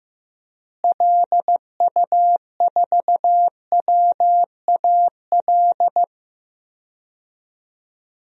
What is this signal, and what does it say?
Morse code, heard as a keyed tone, about 15 wpm, 700 Hz. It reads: LU4WAL
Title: DXCC-100 Call Signs - 6 Characters - Repeated 3x - 15wpm